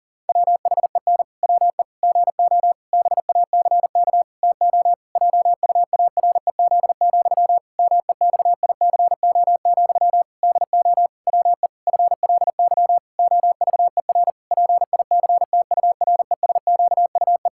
Morse code, heard as a keyed tone, about 40 wpm, 700 Hz. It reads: WHEN WE GO BACK TO JUAREZ, MEXICO, DO WE FLY OVER PICTURESQUE
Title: Pangram Challenge v2 - 40wpm